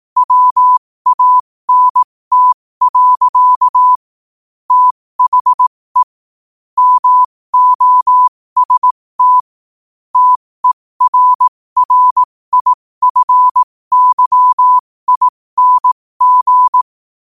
Morse code, heard as a keyed tone, about 18 wpm, spaced as for 15 wpm, 1000 Hz. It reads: WANT. THE MOST TERRIFYING
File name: qotdfiles / 15wpm